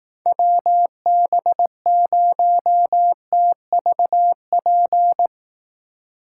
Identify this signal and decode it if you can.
Morse code, heard as a keyed tone, about 18 wpm, 700 Hz. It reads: WB0TVP
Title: Speed Racing - USA 2x3 Call Signs 15wpm